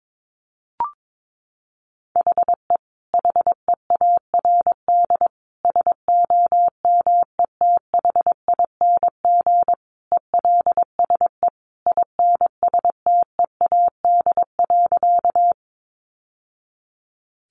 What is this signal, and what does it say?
Morse code, heard as a keyed tone, about 22 wpm, 700 Hz. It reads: HE HEARD SOMETHING ELSE INSTEAD.
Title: Sentences from Top 800 Words 22wpm